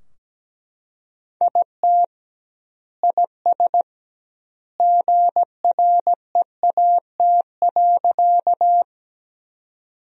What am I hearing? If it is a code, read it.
Morse code, heard as a keyed tone, about 17 wpm, 700 Hz. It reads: IT IS GREAT.
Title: 2x Word-Spacing - Sentences from Top 200 Words - 17wpm